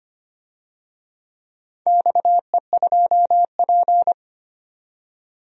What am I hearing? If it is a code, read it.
Morse code, heard as a keyed tone, about 25 wpm, 700 Hz. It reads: XE2P
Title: DXCC-100 Call Signs - 4 Characters - 25wpm